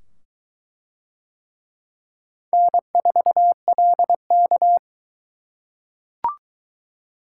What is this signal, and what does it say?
Morse code, heard as a keyed tone, about 21 wpm, 700 Hz. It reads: N4LK